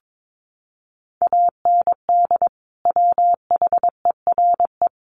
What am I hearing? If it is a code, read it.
Morse code, heard as a keyed tone, about 22 wpm, 700 Hz. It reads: AND WHERE
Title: Sentences from Top 200 Words 22wpm